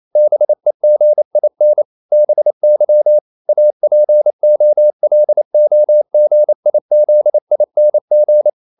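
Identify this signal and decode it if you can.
Morse code, heard as a keyed tone, about 28 wpm, 600 Hz. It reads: BEGIN BY APOLOGIZING
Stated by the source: Morse code practice at 28 wpm, text of Ken Follett's novel Never in English, from CW med Gnister.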